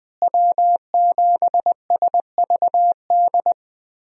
Morse code, heard as a keyed tone, about 20 wpm, 700 Hz. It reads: W7SVD
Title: USA Call Signs - 5 Characters - Repeated 2x - 20wpm